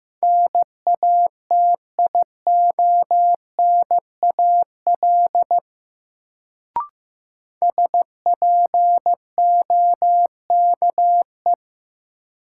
Morse code, heard as a keyed tone, about 16 wpm, 700 Hz. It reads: NATIONAL SPOKE
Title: Sentences from Top 1200 Words - New Word Drill Alt - 15wpm